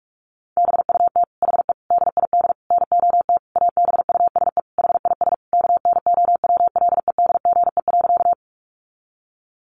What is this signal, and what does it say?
Morse code, heard as a keyed tone, about 44 wpm, 700 Hz. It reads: BUT HE DID NOT ABUSE HIS KNOWLEDGE.
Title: Sentences from Top 1300 Words 45wpm